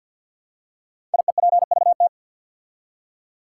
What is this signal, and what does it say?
Morse code, heard as a keyed tone, about 50 wpm, 700 Hz. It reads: IEPVT
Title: Code Groups - 5 Letters - 50wpm